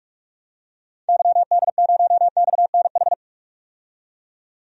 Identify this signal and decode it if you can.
Morse code, heard as a keyed tone, about 45 wpm, 700 Hz. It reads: YD0XNH
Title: DXCC-100 Call Signs - 6 Characters - 45wpm